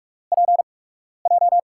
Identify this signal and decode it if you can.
Morse code, heard as a keyed tone, about 44 wpm, spaced as for 12 wpm, 700 Hz. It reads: PJ
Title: USA Call Signs - 5 Characters - 8x Character Spacing - 45wpm